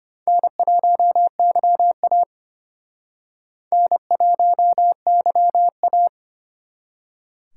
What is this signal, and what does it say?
Morse code, heard as a keyed tone, about 32 wpm, 700 Hz. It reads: N1YA N1YA